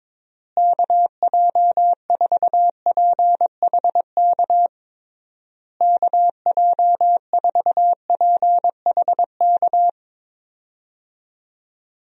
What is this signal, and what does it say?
Morse code, heard as a keyed tone, about 22 wpm, 700 Hz. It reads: KJ4PHK KJ4PHK
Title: USA Call Signs - 6 Characters - Repeated 3x - 22wpm